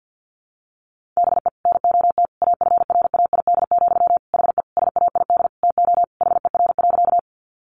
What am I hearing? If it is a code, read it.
Morse code, heard as a keyed tone, about 50 wpm, 700 Hz. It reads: BE NOT AFRAID, HE SAID TO HER.
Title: Sentences from Top 1500 Words 50wpm